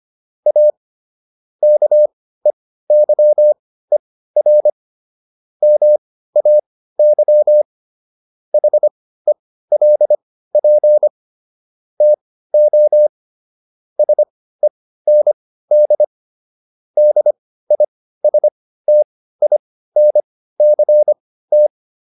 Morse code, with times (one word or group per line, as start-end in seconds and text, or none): 0.5-0.7 A
1.6-4.7 KEYER
5.6-7.6 MAY
8.5-11.1 HELP
12.0-13.1 TO
14.0-16.0 SEND
17.0-21.7 DISTINCT